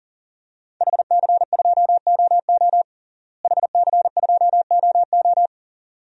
Morse code, heard as a keyed tone, about 37 wpm, 700 Hz. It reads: HC2OO HC2OO
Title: DXCC-100 Call Signs - 5 Characters - Repeated 2x - 40wpm